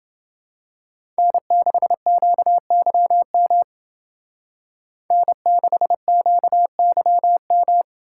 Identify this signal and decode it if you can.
Morse code, heard as a keyed tone, about 28 wpm, 700 Hz. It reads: N6QYM N6QYM